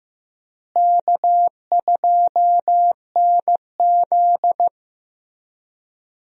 Morse code, heard as a keyed tone, about 15 wpm, 700 Hz. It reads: K2NZ
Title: USA Call Signs - 4 Characters - Repeated 3x - 15wpm